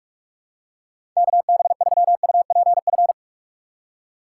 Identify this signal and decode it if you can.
Morse code, heard as a keyed tone, about 46 wpm, 700 Hz. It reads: KB3UPF
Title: USA Call Signs - 6 Characters - 4x Character Spacing - 45wpm